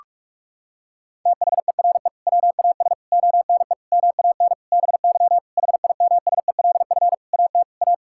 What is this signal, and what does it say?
Morse code, heard as a keyed tone, about 46 wpm, 700 Hz. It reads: THERE WAS ONE MAN BY HIMSELF AT A
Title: Sentences from Top 1600 Words 45wpm